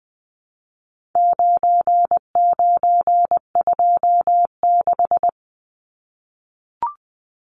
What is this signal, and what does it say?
Morse code, heard as a keyed tone, about 20 wpm, 700 Hz. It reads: 9926
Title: Numbers - 4-Digit 20wpm